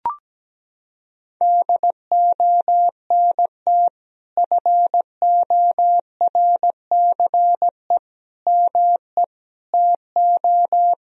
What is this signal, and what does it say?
Morse code, heard as a keyed tone, about 17 wpm, 700 Hz. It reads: DONT FORCE ME TO
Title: Sentences from Top 1000 Words - Review Part 3 - 17wpm